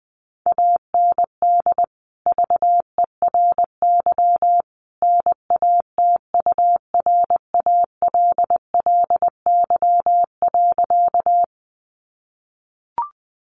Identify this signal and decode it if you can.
Morse code, heard as a keyed tone, about 20 wpm, 700 Hz. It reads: AND VERY NATURALLY.
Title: Sentences from Top 1300 Words 20wpm